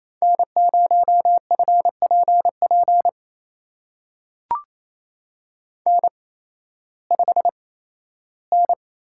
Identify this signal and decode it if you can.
Morse code, heard as a keyed tone, about 28 wpm, spaced as for 15 wpm, 700 Hz. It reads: N0FPP N5N
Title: USA Call Signs - 5 Characters - 8x Character Spacing - 28wpm